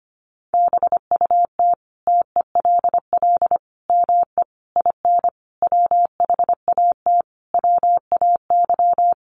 Morse code, heard as a keyed tone, about 25 wpm, 700 Hz. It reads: BUT TELL ME IN WHAT WAY
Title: Sentences from Top 1300 Words 25wpm